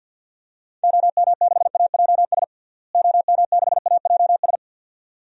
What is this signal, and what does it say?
Morse code, heard as a keyed tone, about 51 wpm, 700 Hz. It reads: OM6AJS OM6AJS
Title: DXCC-100 Call Signs - 6 Characters - Repeated 2x - 50wpm